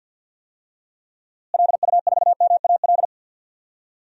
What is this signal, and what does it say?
Morse code, heard as a keyed tone, about 50 wpm, 700 Hz. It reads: LU4MAL
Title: DXCC-100 Call Signs - 6 Characters - Repeated 3x - 50wpm